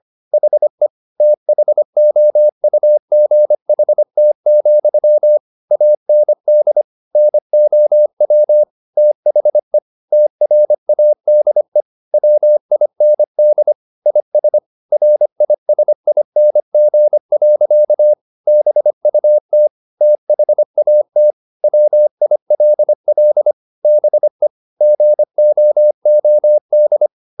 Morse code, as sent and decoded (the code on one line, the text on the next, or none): .... / - .... --- ..- --. .... - --..-- / .- -. -.. / -. --- .-- / - .... / - .-. .- -.. / .-- .. -. -.. / .. ... / .-. .. ... .. -. --. .-.-.- / -... ..- - / - .... .- - / .-- .. .-.. .-.. / -... / --. --- --- -..
HE THOUGHT, AND NOW THE TRADE WIND IS RISING. BUT THAT WILL BE GOOD